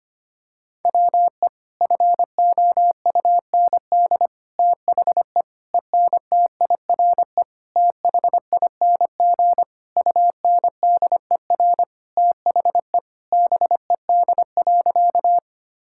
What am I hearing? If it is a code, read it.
Morse code, heard as a keyed tone, about 25 wpm, 700 Hz. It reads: WE FOUND THE ENTIRE THING UNDER THE BED.
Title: Sentences from Top 1000 Words - Review Part 5 - 25wpm